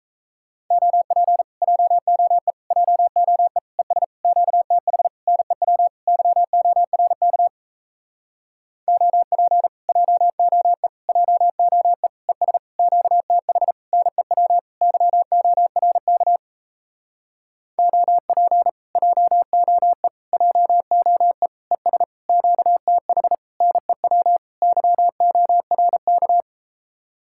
Morse code, heard as a keyed tone, about 37 wpm, 700 Hz. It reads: OP JOE JOE ES QTH NEW YORK OP JOE JOE ES QTH NEW YORK OP JOE JOE ES QTH NEW YORK